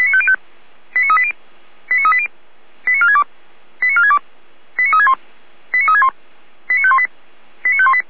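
Five-tone selective call (ZVEI), heard as quick runs of five tones, tones of about 70 ms. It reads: 89380 89380 89462 89462 89361 89361 89518 89518
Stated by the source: ZVEI five-tone selective call 89022 received on SDR by feuerwehr-orlamuende (fire brigade).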